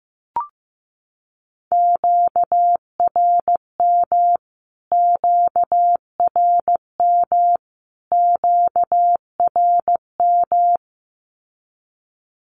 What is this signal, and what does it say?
Morse code, heard as a keyed tone, about 15 wpm, 700 Hz. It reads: QRM QRM QRM
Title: Q-Codes Repeated 3x 15wpm